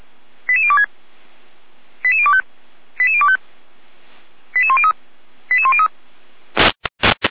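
Five-tone selective call (ZVEI), heard as quick runs of five tones, tones of about 70 ms. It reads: 88026 88025 88025 88103 88103
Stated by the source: ZVEI five-tone call received by SDR, from feuerwehr-orlamuende.org (fire brigade Orlamünde).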